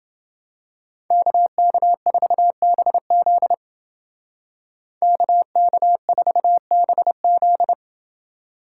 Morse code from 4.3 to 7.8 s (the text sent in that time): KK4BZ